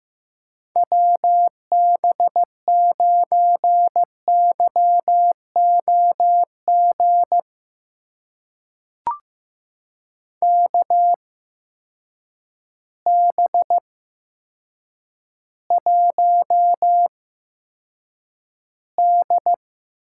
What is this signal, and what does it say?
Morse code, heard as a keyed tone, about 15 wpm, spaced as for 8 wpm, 700 Hz. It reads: WB9YOG KB1D
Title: USA Call Signs - 6 Characters - 8x Character Spacing - 15wpm